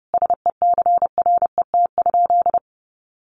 Morse code, heard as a keyed tone, about 30 wpm, 700 Hz. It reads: SECRET?
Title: Sentences from Top 1200 Words 30wpm